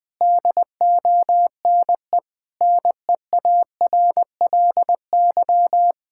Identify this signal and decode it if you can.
Morse code, heard as a keyed tone, about 19 wpm, 700 Hz. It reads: DONE NEARLY
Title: Sentences from Top 1600 Words 20wpm